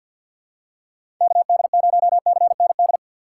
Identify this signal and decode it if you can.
Morse code, heard as a keyed tone, about 51 wpm, 700 Hz. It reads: KD0CND
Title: USA Call Signs - 6x Character Spacing - 50wpm